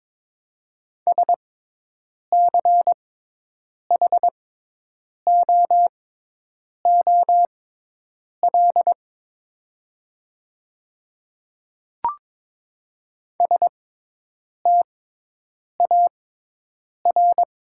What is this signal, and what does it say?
Morse code, heard as a keyed tone, about 22 wpm, spaced as for 8 wpm, 700 Hz. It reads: SCHOOL STAR
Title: 6x Character Spacing - Top 500 Words - 22wpm